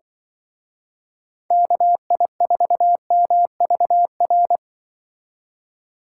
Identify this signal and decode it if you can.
Morse code, heard as a keyed tone, about 24 wpm, 700 Hz. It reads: KI4MVR
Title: Speed Racing - USA 2x3 Call Signs 20wpm